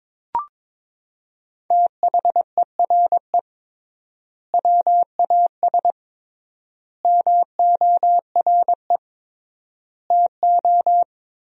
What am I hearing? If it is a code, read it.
Morse code, heard as a keyed tone, about 22 wpm, 700 Hz. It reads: THERE WAS MORE TO
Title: 3x Word-Spacing - Sentences from Top 500 Words - 22wpm